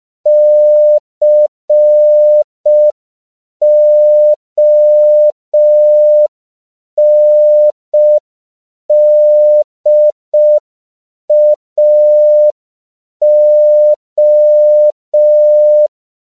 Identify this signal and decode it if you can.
Morse code, heard as a keyed tone, about 5 wpm, 600 Hz. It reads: CONDAO